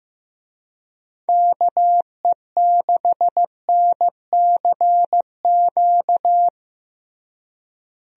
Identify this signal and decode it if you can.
Morse code, heard as a keyed tone, about 15 wpm, 700 Hz. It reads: KE6NCQ
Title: USA Call Signs - 6 Characters - Repeated 2x - 15wpm